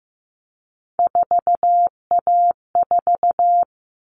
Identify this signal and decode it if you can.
Morse code, heard as a keyed tone, about 15 wpm, 700 Hz. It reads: 4A4
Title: All Cut Numbers - 3-Digit 15wpm